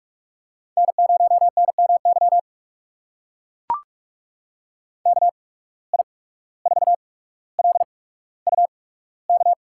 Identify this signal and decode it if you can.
Morse code, heard as a keyed tone, about 44 wpm, spaced as for 18 wpm, 700 Hz. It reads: N0NMY KI4LUK
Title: USA Call Signs - 8x Character Spacing - 45wpm